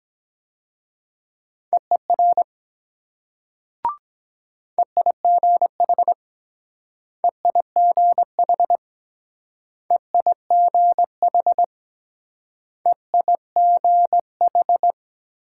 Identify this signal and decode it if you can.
Morse code, heard as a keyed tone, about 21 wpm, 700 Hz. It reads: EER EIGH EIGH EIGH EIGH